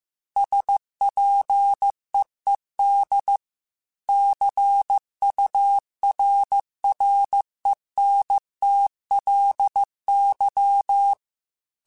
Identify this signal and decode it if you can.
Morse code, heard as a keyed tone, about 15 wpm, 800 Hz. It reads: SPEED CURRENTLY
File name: Weather Sets / Weather Sets-mp3s / Weather 5 5 @13